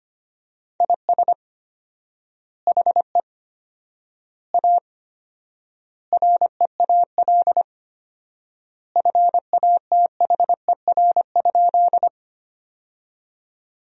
Morse code, heard as a keyed tone, about 25 wpm, 700 Hz. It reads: IS HE A REAL FATHER?
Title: 4x Word-Spacing - Sentences from Top 300 Words - 25wpm